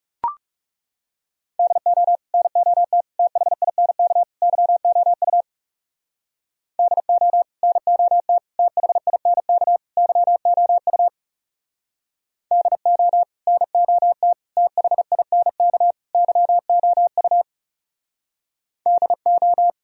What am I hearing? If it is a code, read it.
Morse code, heard as a keyed tone, about 35 wpm, 700 Hz. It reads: DO NOT THINK YOU DO NOT THINK YOU DO NOT THINK YOU DO
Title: Speed-Racing - Sets of 4 Words 30wpm